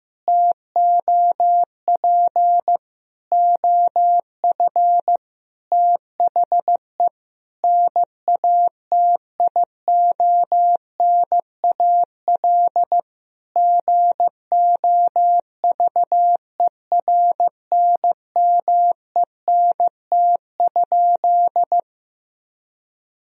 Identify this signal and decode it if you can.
Morse code, heard as a keyed tone, about 15 wpm, 700 Hz. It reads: TOP OF THE NATIONAL GOVERNMENT?
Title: Sentences from Top 1000 Words - Review Part 4 - 15wpm